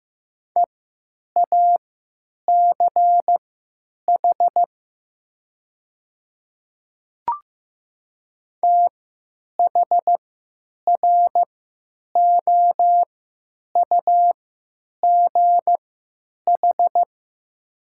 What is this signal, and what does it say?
Morse code, heard as a keyed tone, about 15 wpm, spaced as for 9 wpm, 700 Hz. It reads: EACH THROUGH
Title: 3x Character Spacing - Top 200 Words - 15wpm